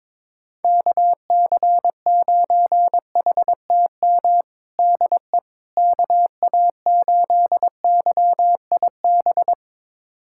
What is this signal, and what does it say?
Morse code, heard as a keyed tone, about 23 wpm, 700 Hz. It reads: KC9HTM DE KA8YIB
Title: USA Callsign DE Callsign 22wpm